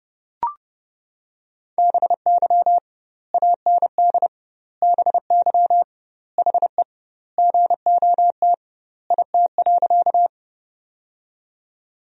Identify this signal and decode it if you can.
Morse code, heard as a keyed tone, about 30 wpm, 700 Hz. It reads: BY AND BY HE GOT IT.
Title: 2x Word-Spacing - Sentences from Top 300 Words - 30wpm